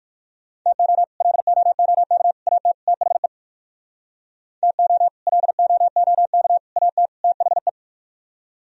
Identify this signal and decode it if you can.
Morse code, heard as a keyed tone, about 59 wpm, 700 Hz. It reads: TO LOOK AT THE TO LOOK AT THE